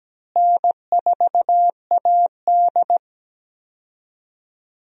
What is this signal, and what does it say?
Morse code, heard as a keyed tone, about 17 wpm, 700 Hz. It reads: N4AD
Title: USA Call Signs - 4 Characters - Repeated 3x - 17wpm